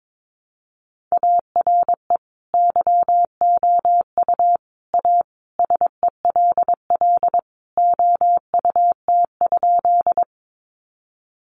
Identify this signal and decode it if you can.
Morse code, heard as a keyed tone, about 22 wpm, 700 Hz. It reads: ARE YOU A SELL OUT?
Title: Sentences from Top 1100 Words 22wpm